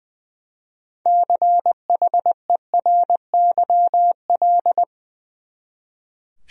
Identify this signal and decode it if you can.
Morse code, heard as a keyed tone, about 20 wpm, 700 Hz. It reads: CHERYL
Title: USA Common Names - 20wpm